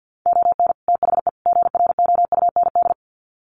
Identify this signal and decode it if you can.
Morse code, heard as a keyed tone, about 48 wpm, 700 Hz. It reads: ON THE GROUND
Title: Sets of 3 Words 50wpm